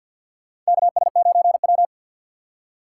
Morse code, heard as a keyed tone, about 51 wpm, 700 Hz. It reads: KS9W
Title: K1USN SST QSO Practice - 50wpm